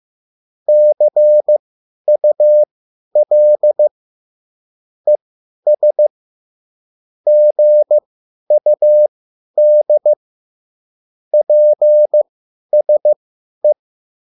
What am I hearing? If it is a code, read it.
Morse code, heard as a keyed tone, about 15 wpm, spaced as for 10 wpm, 600 Hz. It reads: CUL ES GUD PSE
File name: QSL / CW lesson